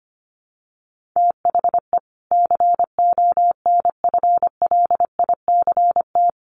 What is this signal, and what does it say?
Morse code, heard as a keyed tone, about 25 wpm, 700 Hz. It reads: THE CONFLICT
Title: Sentences from Top 1300 Words 25wpm